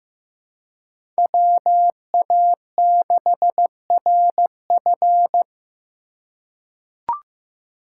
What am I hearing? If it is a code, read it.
Morse code, heard as a keyed tone, about 16 wpm, 700 Hz. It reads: WA6RF
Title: USA Call Signs - 5 Characters - 4x Character Spacing - 15wpm